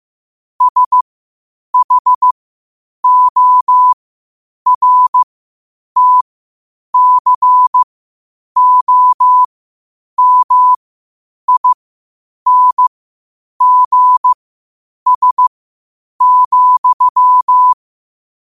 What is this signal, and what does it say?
Morse code, heard as a keyed tone, about 15 wpm, spaced as for 8 wpm, 1000 Hz. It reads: SHORTCOMINGS,
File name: qotdfiles / 8wpm